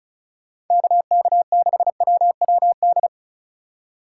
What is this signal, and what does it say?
Morse code, heard as a keyed tone, about 45 wpm, 700 Hz. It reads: KK6WWD